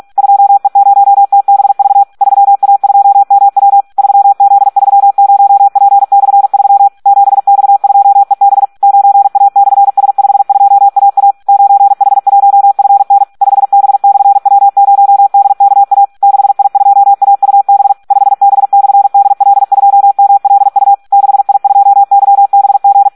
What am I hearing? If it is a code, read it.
Morse code, heard as a keyed tone, about 46 wpm, 800 Hz. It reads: 2E0TBV 3A2MW 3Z30P/3 7X2EB 9A/S52AA 9H1FN 5B/W0DKA 6I2AUB 5B/DL2MLU 6I2YBG
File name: Audio practica CW / SD 1 46